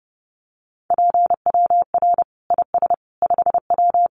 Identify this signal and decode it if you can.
Morse code, heard as a keyed tone, about 30 wpm, 700 Hz. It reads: PWR IS 5W
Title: Common QSO Phrases 30wpm